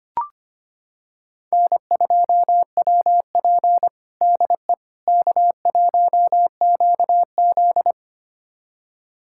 Morse code, heard as a keyed tone, about 25 wpm, 700 Hz. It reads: N2WP DE K1QZ
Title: USA Callsign DE Callsign - 4 Characters - 25wpm